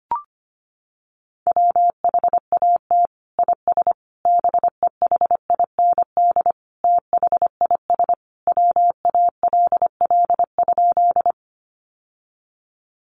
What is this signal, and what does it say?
Morse code, heard as a keyed tone, about 25 wpm, 700 Hz. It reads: WHAT IS BEHIND THIS WALL?
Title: Sentences from Top 1100 Words 25wpm